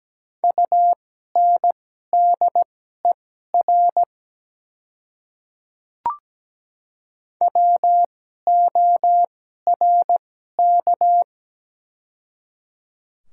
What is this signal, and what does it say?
Morse code, heard as a keyed tone, about 17 wpm, spaced as for 12 wpm, 700 Hz. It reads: UNDER WORK
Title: 2x Character Spacing - Top 200 Words - 17wpm